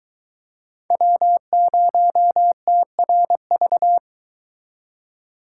Morse code, heard as a keyed tone, about 24 wpm, 700 Hz. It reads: W0TRV